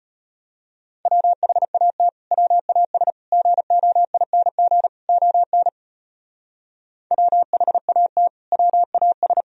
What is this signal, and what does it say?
Morse code, heard as a keyed tone, about 32 wpm, 700 Hz. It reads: WHAT WAS GOING ON WHAT WAS